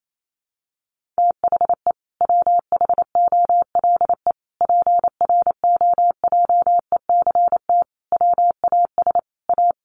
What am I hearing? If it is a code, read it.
Morse code, heard as a keyed tone, about 28 wpm, 700 Hz. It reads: THE WHOLE PROJECT WAS A
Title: Sentences from Top 1400 Words 28wpm